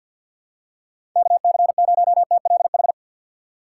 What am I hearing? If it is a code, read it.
Morse code, heard as a keyed tone, about 53 wpm, 700 Hz. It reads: KC0TLH
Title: USA Call Signs - 4x Character Spacing - 50wpm